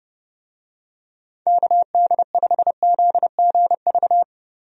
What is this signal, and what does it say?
Morse code, heard as a keyed tone, about 30 wpm, 700 Hz. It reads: KD5ZGV
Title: USA Call Signs - 6 Characters - Repeated 2x - 30wpm